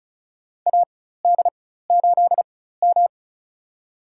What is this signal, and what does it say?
Morse code, heard as a keyed tone, about 34 wpm, spaced as for 16 wpm, 690 Hz. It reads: AD8M
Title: USA Call Signs - 4x Character Spacing - 35wpm